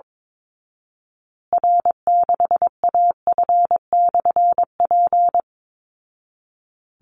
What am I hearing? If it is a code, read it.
Morse code, heard as a keyed tone, about 22 wpm, 700 Hz. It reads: R6AF/P
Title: Call Signs Repeated 3x - 22wpm